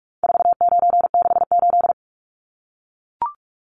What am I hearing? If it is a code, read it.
Morse code, heard as a keyed tone, about 44 wpm, 700 Hz. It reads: 4968
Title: Numbers - 4-Digit 45wpm